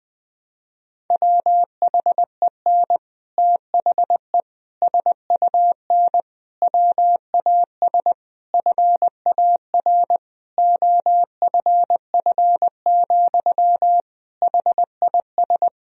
Morse code, heard as a keyed tone, about 20 wpm, 700 Hz. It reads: WHEN THE SUN WAS FAR OFF, HIS
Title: Sentences from Top 1000 Words - Review Part 3 - 20wpm